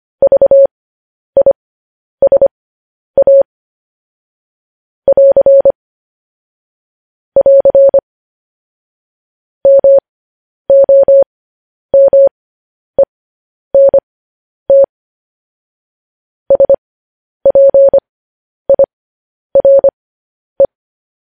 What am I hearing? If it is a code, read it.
Morse code, heard as a keyed tone, about 25 wpm, spaced as for 10 wpm, 550 Hz. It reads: VISA <AR> <AR> MOMENT SPIRE
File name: podcast / News headlines-1x-25x10